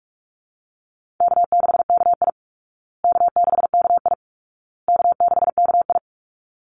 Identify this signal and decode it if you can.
Morse code, heard as a keyed tone, about 44 wpm, 700 Hz. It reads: K6KI K6KI K6KI
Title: Easy Call Signs Repeated 3x - 45wpm